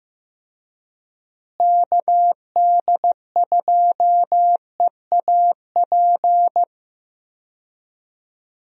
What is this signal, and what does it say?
Morse code, heard as a keyed tone, about 16 wpm, 700 Hz. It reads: KD2EAP
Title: USA Call Signs - 6 Characters - Repeated 2x - 15wpm